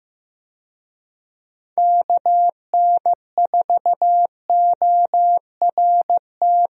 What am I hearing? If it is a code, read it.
Morse code, heard as a keyed tone, about 16 wpm, 700 Hz. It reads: KN4ORT
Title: USA Call Signs - 6 Characters - Repeated 2x - 15wpm